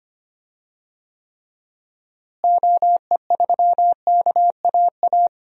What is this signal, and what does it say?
Morse code, heard as a keyed tone, about 25 wpm, 700 Hz. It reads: OE3KAA
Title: DXCC-100 Call Signs - 6 Characters - Repeated 2x - 25wpm